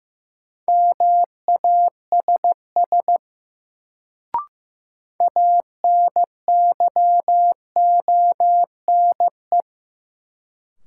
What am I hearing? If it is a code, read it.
Morse code, heard as a keyed tone, about 15 wpm, 700 Hz. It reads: MASS ANYONE
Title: Sets of 5 Words Encore - New Word Drill - 15wpm